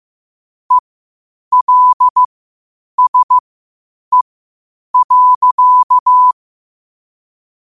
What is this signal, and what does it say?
Morse code, heard as a keyed tone, about 15 wpm, spaced as for 8 wpm, 1000 Hz. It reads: ELSE.
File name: 8wpm